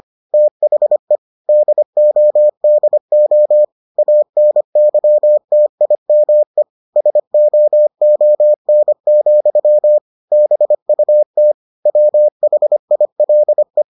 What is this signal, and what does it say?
Morse code, heard as a keyed tone, about 25 wpm, 600 Hz. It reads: THE DODO ANYTIME SOON, BUT WHILE